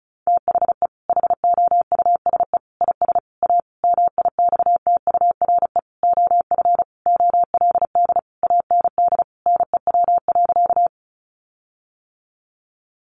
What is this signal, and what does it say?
Morse code, heard as a keyed tone, about 34 wpm, 700 Hz. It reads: THE HOUSE IS A MIXTURE OF OLD AND NEW.
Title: Sentences from Top 1500 Words 35wpm